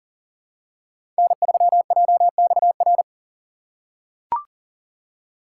Morse code, heard as a keyed tone, about 40 wpm, 700 Hz. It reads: N3JXR
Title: USA Call Signs - 5 Characters - 3x Character Spacing - 40wpm